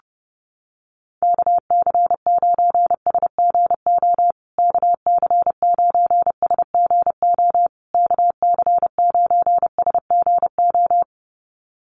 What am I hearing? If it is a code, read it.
Morse code, heard as a keyed tone, about 30 wpm, 700 Hz. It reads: KC9SGO KC9SGO KC9SGO
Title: USA Call Signs Repeated 3x 30wpm